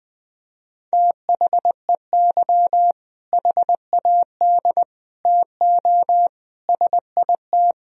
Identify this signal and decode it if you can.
Morse code, heard as a keyed tone, about 21 wpm, 700 Hz. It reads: THEY HAD TO SIT
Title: Sentences from Top 1000 Words - Review Part 1 - 20wpm